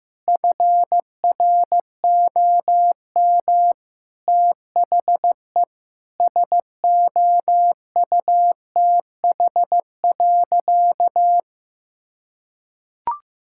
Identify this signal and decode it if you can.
Morse code, heard as a keyed tone, about 15 wpm, 700 Hz. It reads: FROM THE SOUTH.
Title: Sentences from Top 1000 Words - Review Part 6 - 15wpm